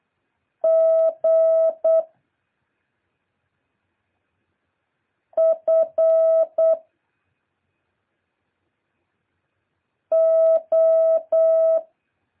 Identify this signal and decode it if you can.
Morse code, heard as a keyed tone, about 8 wpm, 650 Hz. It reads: G F O
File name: jh8cbh / 2023cw